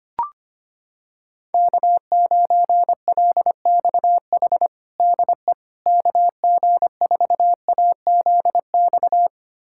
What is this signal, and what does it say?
Morse code, heard as a keyed tone, about 25 wpm, 700 Hz. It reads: K9LXH DE KG4AZX
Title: USA Callsign DE Callsign 25wpm